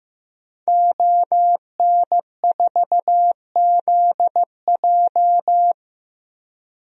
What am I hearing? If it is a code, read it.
Morse code, heard as a keyed tone, about 15 wpm, 700 Hz. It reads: ON4ZJ
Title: DXCC-100 Call Signs - 5 Characters - Repeated 2x - 15wpm